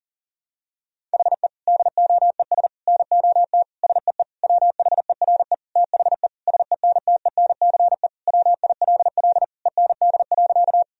Morse code, heard as a keyed tone, about 40 wpm, 700 Hz. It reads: HE DOES NOT SEE WHERE THE SENTENCE WILL END.
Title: Sentences from Top 1000 Words - Review Part 1 - 40wpm